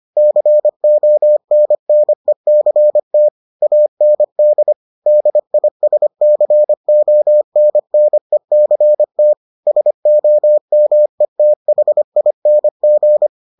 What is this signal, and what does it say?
Morse code, heard as a keyed tone, about 25 wpm, 600 Hz. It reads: CONNECT AND DISCONNECT SOMETHING